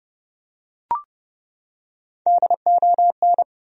Morse code, heard as a keyed tone, about 30 wpm, 700 Hz. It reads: DON
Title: USA Common Names - Max 3 Letters - 30wpm